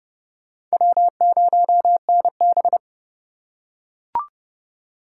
Morse code, heard as a keyed tone, about 26 wpm, 700 Hz. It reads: W0NB